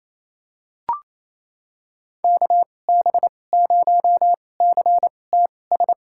Morse code, heard as a keyed tone, about 28 wpm, spaced as for 20 wpm, 700 Hz. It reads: KB0CTS
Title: USA Call Signs - 6 Characters - 2x Character Spacing - 28wpm